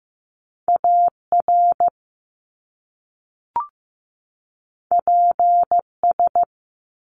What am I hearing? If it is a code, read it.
Morse code, heard as a keyed tone, about 15 wpm, 700 Hz. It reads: AR PS
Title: Top Two Letter Combinations in Words 15wpm